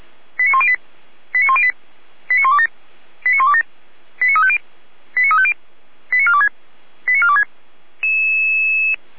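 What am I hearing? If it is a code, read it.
Five-tone selective call (ZVEI), heard as quick runs of five tones, tones of about 70 ms. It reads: 89108 89108 89127 89127 89350 89350 89426 89426
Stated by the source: ZVEI five-tone selective call received with an SDR.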